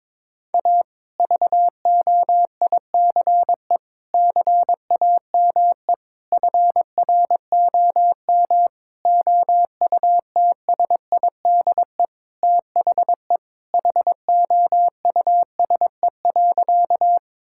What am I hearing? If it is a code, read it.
Morse code, heard as a keyed tone, about 22 wpm, 700 Hz. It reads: A VOICE CAME FROM OUTSIDE THE HOUSE.
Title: Sentences from Top 1000 Words - Review Part 6 - 22wpm